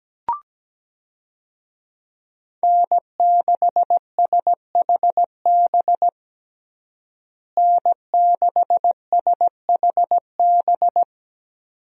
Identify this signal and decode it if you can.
Morse code, heard as a keyed tone, about 17 wpm, 700 Hz. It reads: N6SHB N6SHB
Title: USA Call Signs - 5 Characters - Repeated 2x - 17wpm